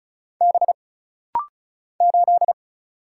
Morse code, heard as a keyed tone, about 36 wpm, 700 Hz. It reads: B 8
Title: Single Letter-Number Alt - Rapid-Fire 35wpm